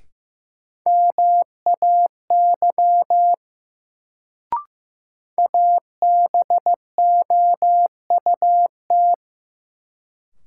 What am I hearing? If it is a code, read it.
Morse code, heard as a keyed tone, about 16 wpm, 700 Hz. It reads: MAY ABOUT
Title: Sets of 2 Words - New Word Drill - 15wpm